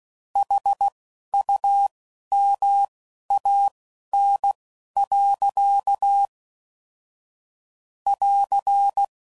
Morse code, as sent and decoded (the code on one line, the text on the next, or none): .... ..- -- .- -. .-.-.- / .-.-.
HUMAN. <AR>